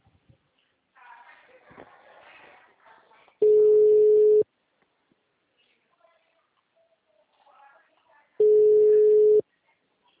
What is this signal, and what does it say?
Ringback tone (Europe, 425 Hz), with speech.